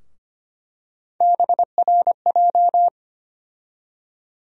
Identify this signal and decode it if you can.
Morse code, heard as a keyed tone, about 25 wpm, 700 Hz. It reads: BRJ
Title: USA Call Sign Postfixes - 3 Letters 25wpm